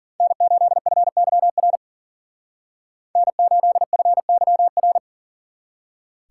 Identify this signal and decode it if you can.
Morse code, heard as a keyed tone, about 52 wpm, 700 Hz. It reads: N8FYR N8FYR